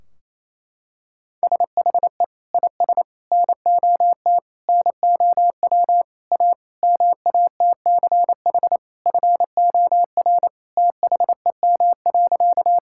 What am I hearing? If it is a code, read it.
Morse code, heard as a keyed tone, about 27 wpm, 700 Hz. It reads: SHE IS NOT NOW A MATCH FOR THEM.
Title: Sentences from Top 1600 Words 28wpm